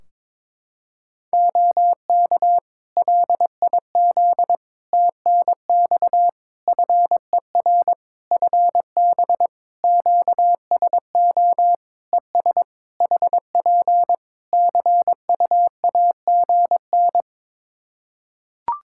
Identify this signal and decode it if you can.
Morse code, heard as a keyed tone, about 22 wpm, 700 Hz. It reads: OK LIZ TNX FER FB QSO ES HP CUAGN
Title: Practice QSO - Exchange 4 - Ending - 22wpm